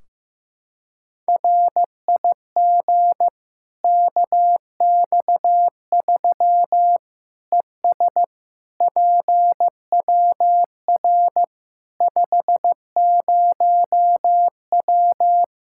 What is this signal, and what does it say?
Morse code, heard as a keyed tone, about 15 wpm, 700 Hz. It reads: RIG KX3 ES PWR 50W
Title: Practice QSO - Exchange 2 - Info - 15wpm